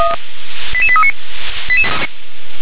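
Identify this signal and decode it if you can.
Five-tone selective call (ZVEI), heard as quick runs of five tones, tones of about 70 ms. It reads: 88839
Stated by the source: Five-tone ZVEI selective call received by SDR.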